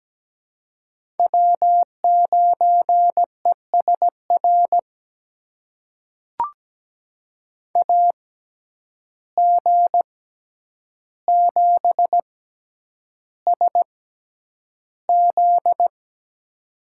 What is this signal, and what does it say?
Morse code, heard as a keyed tone, about 17 wpm, spaced as for 9 wpm, 700 Hz. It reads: W9ESR AG7SZ
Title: USA Call Signs - 6x Character Spacing - 17wpm